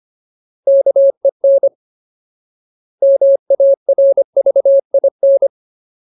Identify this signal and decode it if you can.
Morse code, heard as a keyed tone, about 25 wpm, 550 Hz. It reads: KEN MARVIN
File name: Ultimate Word List / 25WPM Ultimate